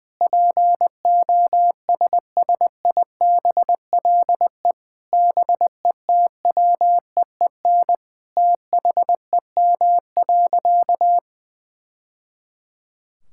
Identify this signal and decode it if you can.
Morse code, heard as a keyed tone, about 21 wpm, 700 Hz. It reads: POSSIBLE BETWEEN THEM.
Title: Sentences from Top 1000 Words - Review Part 5 - 20wpm